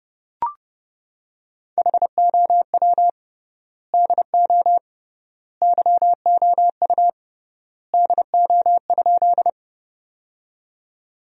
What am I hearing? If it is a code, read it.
Morse code, heard as a keyed tone, about 30 wpm, 700 Hz. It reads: HOW DO YOU DO?
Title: 3x Word-Spacing - Sentences from Top 100 Words - 30wpm